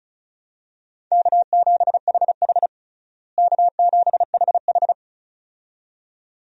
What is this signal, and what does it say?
Morse code, heard as a keyed tone, about 36 wpm, 700 Hz. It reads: K7HH K7HH
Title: USA Call Signs - 4 Characters - Repeated 2x - 35wpm